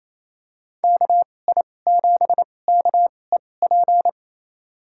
Morse code, heard as a keyed tone, about 28 wpm, spaced as for 20 wpm, 700 Hz. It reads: KI7KEP
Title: USA Call Signs - 6 Characters - 2x Character Spacing - 28wpm